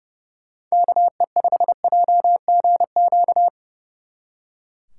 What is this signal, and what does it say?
Morse code, heard as a keyed tone, about 36 wpm, 700 Hz. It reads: KE5JGQ
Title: Speed Racing - USA 2x3 Call Signs 30wpm